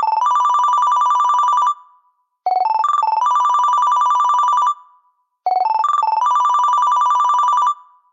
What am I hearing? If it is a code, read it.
There is no code.